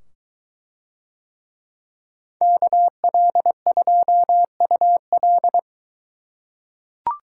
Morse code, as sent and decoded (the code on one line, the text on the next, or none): -.- .-.. ..--- ..- .-..
KL2UL